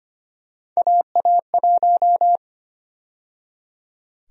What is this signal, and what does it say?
Morse code, heard as a keyed tone, about 25 wpm, 700 Hz. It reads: AA1
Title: USA Call Sign Prefixes - 25wpm